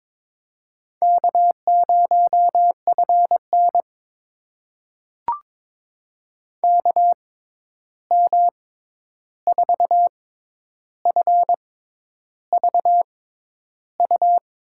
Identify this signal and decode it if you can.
Morse code, heard as a keyed tone, about 22 wpm, spaced as for 10 wpm, 700 Hz. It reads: K0FN KM4FVU
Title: USA Call Signs - 6x Character Spacing - 22wpm